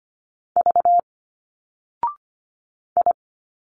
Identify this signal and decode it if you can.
Morse code, heard as a keyed tone, about 25 wpm, 700 Hz. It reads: V I
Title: Single Letters 25wpm